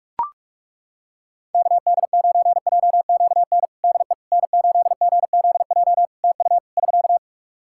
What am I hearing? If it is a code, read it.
Morse code, heard as a keyed tone, about 41 wpm, 700 Hz. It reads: KD9JQN DE N8GZJ TU <SK>
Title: K1USN SST QSO Practice 45wpm